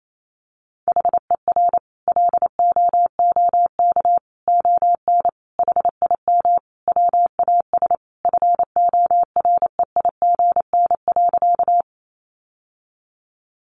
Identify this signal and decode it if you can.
Morse code, heard as a keyed tone, about 28 wpm, 700 Hz. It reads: HER LOOK ON HIM WAS FOREIGN.
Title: Sentences from Top 1400 Words 28wpm